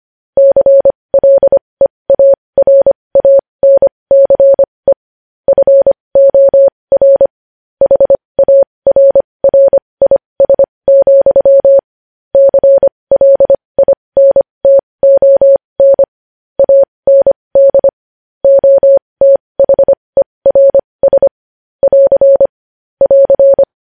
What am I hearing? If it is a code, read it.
Morse code, heard as a keyed tone, about 25 wpm, spaced as for 20 wpm, 550 Hz. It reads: CLEARANCE FOR HARRIS, CLINTON AND OTHERS <AR> <AR>
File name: podcast / News headlines-1x-25x20